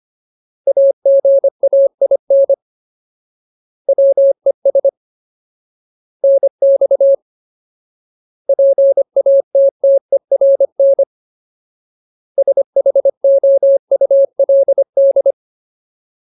Morse code, heard as a keyed tone, about 25 wpm, 550 Hz. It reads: AGAIN WES NX PATTERN SHOULD
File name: Ultimate Word List / 25WPM Ultimate